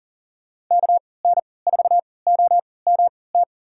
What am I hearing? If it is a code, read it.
Morse code, heard as a keyed tone, about 40 wpm, spaced as for 23 wpm, 700 Hz. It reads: KN4OMT
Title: USA Call Signs - 6 Characters - 3x Character Spacing - 40wpm